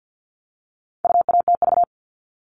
Morse code, heard as a keyed tone, about 50 wpm, 700 Hz. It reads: UATV
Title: All Cut Numbers - 4-Digit 50wpm